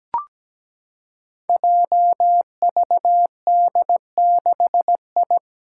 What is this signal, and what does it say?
Morse code, heard as a keyed tone, about 17 wpm, 700 Hz. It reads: JVD6I
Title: Five Challenging Letter-Number Combinations 17wpm